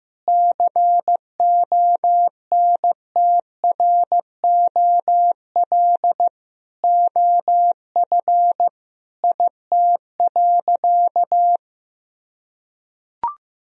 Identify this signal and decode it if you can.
Morse code, heard as a keyed tone, about 16 wpm, 700 Hz. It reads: CONTROL OF IT.
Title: Sentences from Top 1000 Words - Review Part 3 - 15wpm